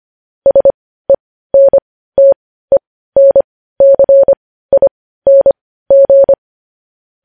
Morse code, heard as a keyed tone, about 25 wpm, spaced as for 15 wpm, 550 Hz. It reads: SENTENCING
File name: podcast / News headlines-1x-25x15